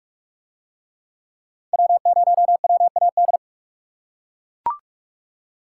Morse code, heard as a keyed tone, about 41 wpm, 700 Hz. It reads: W0WAD